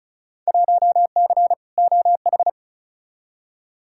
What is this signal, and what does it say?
Morse code, heard as a keyed tone, about 36 wpm, 700 Hz. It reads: 1C OH
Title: Field Day Exchanges - 35wpm